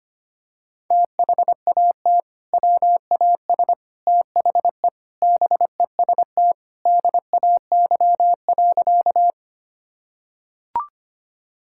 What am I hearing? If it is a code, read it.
Morse code, heard as a keyed tone, about 25 wpm, 700 Hz. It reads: THAT WAS THE BEST DAY.
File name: practice / 25wpm / Sentences with 3-Word Phrases - 25wpm